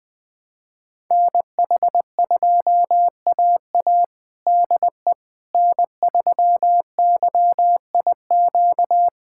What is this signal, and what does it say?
Morse code, heard as a keyed tone, about 21 wpm, 700 Hz. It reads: NH2AA DE N3YIQ
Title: USA Callsign DE Callsign - 5 Characters - 20wpm